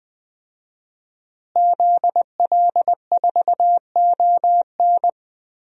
Morse code, heard as a keyed tone, about 20 wpm, 700 Hz. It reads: ZL4ON
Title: DXCC-100 Call Signs - 5 Characters - Repeated 3x - 20wpm